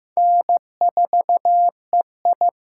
Morse code, heard as a keyed tone, about 15 wpm, 700 Hz. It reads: N4EI
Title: USA Call Signs - 8x Character Spacing - 15wpm